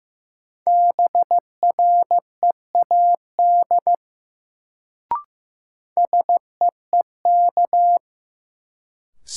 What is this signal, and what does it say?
Morse code, heard as a keyed tone, about 15 wpm, 700 Hz. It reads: BREAD SEEK
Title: Sentences from Top 1600 Words - New Word Drill - 15wpm